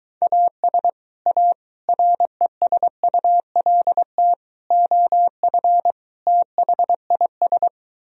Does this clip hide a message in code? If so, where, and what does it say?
Yes: Morse code from 0.2 to 7.7 s, heard as a keyed tone, about 23 wpm, 700 Hz. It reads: AS A RESULT OF THIS